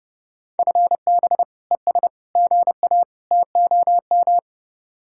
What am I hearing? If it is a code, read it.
Morse code, heard as a keyed tone, about 29 wpm, 700 Hz. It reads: FB ES GA TOM